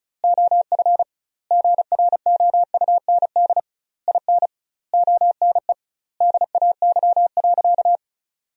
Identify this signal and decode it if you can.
Morse code, heard as a keyed tone, about 36 wpm, 700 Hz. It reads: OF GROUND IN ONE DAY.
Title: 2x Word-Spacing - Sentences from Top 500 Words - 35wpm